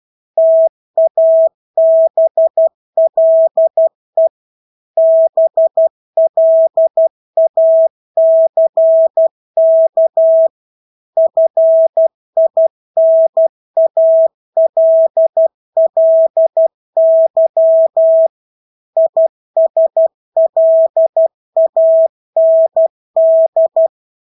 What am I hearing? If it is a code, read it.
Morse code, heard as a keyed tone, about 12 wpm, 650 Hz. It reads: TABLE BLACK FINALLY ISLAND
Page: Engelske ord 12 wpm